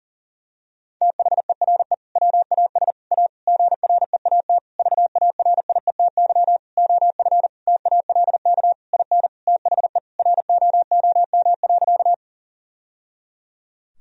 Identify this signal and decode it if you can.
Morse code, heard as a keyed tone, about 37 wpm, 700 Hz. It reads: THERE WAS A GREAT VARIETY OF TALK IN THE ROOM.
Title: Sentences from Top 1000 Words - Review Part 4 - 40wpm